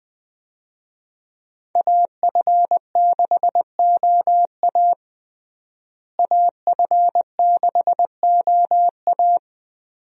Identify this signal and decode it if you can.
Morse code, heard as a keyed tone, about 20 wpm, 700 Hz. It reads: AF6OA AF6OA
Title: USA Call Signs - 5 Characters - Repeated 2x - 20wpm